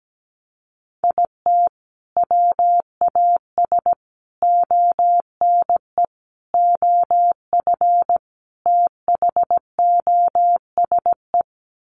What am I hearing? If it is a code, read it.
Morse code, heard as a keyed tone, about 17 wpm, 700 Hz. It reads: IT WAS ONE OF THOSE
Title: Sets of 5 Words Encore 17wpm